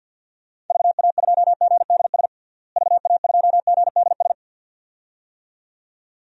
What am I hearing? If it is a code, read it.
Morse code, heard as a keyed tone, about 46 wpm, 700 Hz. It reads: VA2GDS VA2GDS
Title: DXCC-100 Call Signs - 6 Characters - Repeated 2x - 50wpm